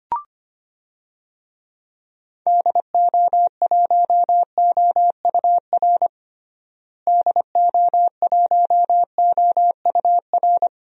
Morse code, heard as a keyed tone, about 25 wpm, 700 Hz. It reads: DO1OUR DO1OUR
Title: DXCC-100 Call Signs - 6 Characters - Repeated 2x - 25wpm